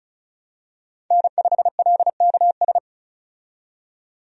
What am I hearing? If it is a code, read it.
Morse code, heard as a keyed tone, about 36 wpm, 700 Hz. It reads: N5LKS
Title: USA Call Signs - 5 Characters - 35wpm